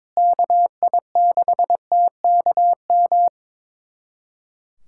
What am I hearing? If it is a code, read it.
Morse code, heard as a keyed tone, about 22 wpm, 700 Hz. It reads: KI6TKM
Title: USA Callsign DE Callsign - 6 Characters - 22wpm